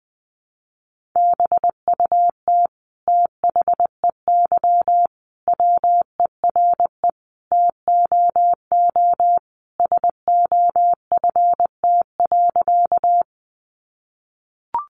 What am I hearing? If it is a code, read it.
Morse code, heard as a keyed tone, about 20 wpm, 700 Hz. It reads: BUT THEY WERE TOO SOFT.
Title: Sentences from Top 1500 Words 20wpm